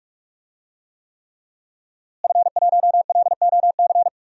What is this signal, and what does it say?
Morse code, heard as a keyed tone, about 45 wpm, 700 Hz. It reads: F1LOC
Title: DXCC-100 Call Signs - 5 Characters - Repeated 3x - 45wpm